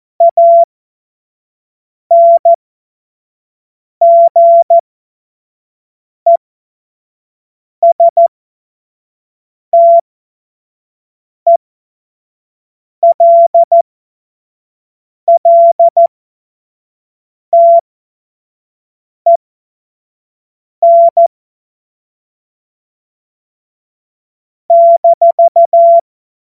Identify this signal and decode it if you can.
Morse code, heard as a keyed tone, about 14 wpm, spaced as for 5 wpm, 680 Hz. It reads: ANGESTELLTEN -
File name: Morsekurs / Witze / Witz 10. Januar 2023 25 F70